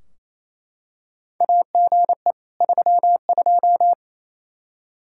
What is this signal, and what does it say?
Morse code, heard as a keyed tone, about 27 wpm, 700 Hz. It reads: AGE 32
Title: K1USN SST QSO Practice 28wpm